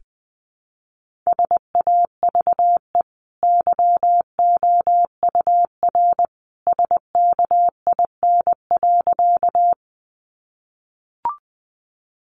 Morse code, heard as a keyed tone, about 20 wpm, 700 Hz. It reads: SAVE YOUR SKIN.
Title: Sentences from Top 1300 Words 20wpm